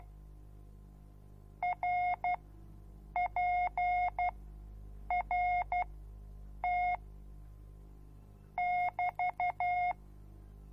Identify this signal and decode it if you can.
Morse code, heard as a keyed tone, about 12 wpm, spaced as for 7 wpm, 720 Hz. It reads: RPRT <BT>